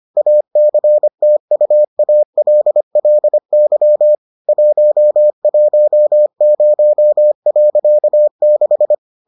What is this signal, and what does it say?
Morse code, heard as a keyed tone, about 25 wpm, 600 Hz. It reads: ACTUALLY 110.6